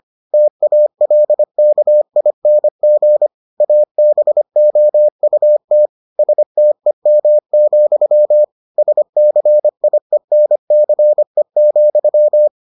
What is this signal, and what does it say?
Morse code, heard as a keyed tone, about 25 wpm, 600 Hz. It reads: TALKING ABOUT STEM, SCIENCE,